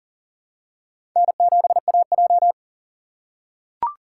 Morse code, heard as a keyed tone, about 40 wpm, 700 Hz. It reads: N7AJ
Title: USA Call Signs - 4 Characters - 3x Character Spacing - 40wpm